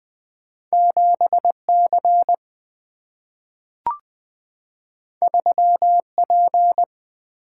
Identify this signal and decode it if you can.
Morse code, heard as a keyed tone, about 20 wpm, 700 Hz. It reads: 7C 3P
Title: Two Letter-Number Combinations - 20wpm